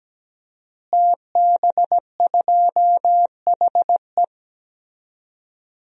Morse code, heard as a keyed tone, about 17 wpm, 700 Hz. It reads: TB2HE
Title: DXCC-100 Call Signs - 5 Characters - Repeated 3x - 17wpm